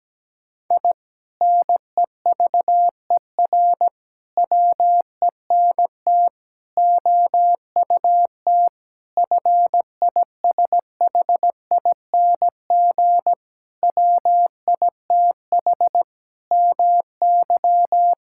Morse code, heard as a keyed tone, about 17 wpm, 700 Hz. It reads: I NEVER WENT OUT FISHING WITH MY
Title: Sentences from Top 1000 Words - Review Part 5 - 17wpm